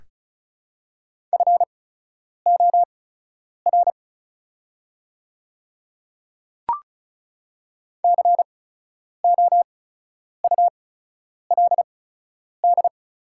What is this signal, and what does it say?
Morse code, heard as a keyed tone, about 36 wpm, spaced as for 10 wpm, 700 Hz. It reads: FOR COULD
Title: 8x Character Spacing - Top 100 Words - 35wpm